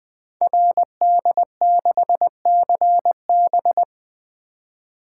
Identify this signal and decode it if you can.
Morse code, heard as a keyed tone, about 21 wpm, 700 Hz. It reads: RD6CB
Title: Code Groups - 5 Letter:Numbers - 20wpm